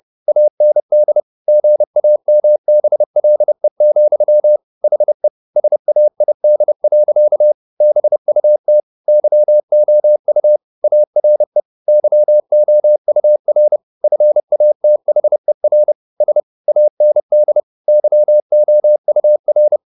Morse code, heard as a keyed tone, about 30 wpm, 600 Hz. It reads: AND GAMBLE, HE SAID. BUT YOU ARE YOUR FATHER S AND YOUR
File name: the old man and the sea / the old man and the sea 1 30wpm